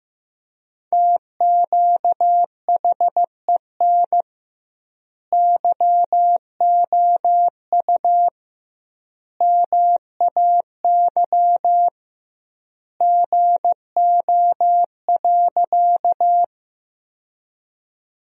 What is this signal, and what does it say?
Morse code, heard as a keyed tone, about 15 wpm, 700 Hz. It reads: TQHEN YOU MAY GO.
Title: 2x Word-Spacing - Sentences from Top 100 Words - 15wpm